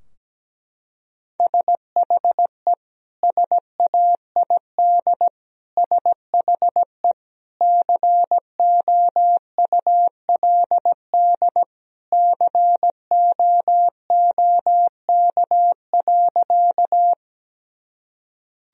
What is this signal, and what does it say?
Morse code, heard as a keyed tone, about 17 wpm, 700 Hz. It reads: SHE SAID SHE COULD COOK.
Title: Sentences from Top 1000 Words - Review Part 4 - 17wpm